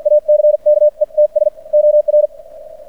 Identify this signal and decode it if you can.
Morse code, heard as a keyed tone, about 56 wpm, spaced as for 44 wpm, 600 Hz. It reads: ACKETS OR
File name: archived / CRSnet